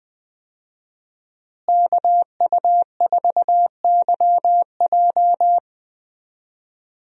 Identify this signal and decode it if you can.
Morse code, heard as a keyed tone, about 20 wpm, 700 Hz. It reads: KU4YJ
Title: USA Call Signs - 5 Characters - Repeated 2x - 20wpm